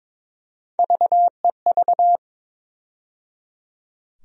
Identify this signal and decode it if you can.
Morse code, heard as a keyed tone, about 22 wpm, 700 Hz. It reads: VEV
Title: USA Call Sign Postfixes - 3 Letters 22wpm